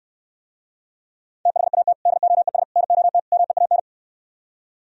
Morse code, heard as a keyed tone, about 68 wpm, 700 Hz. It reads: THAT DOES NOT MEAN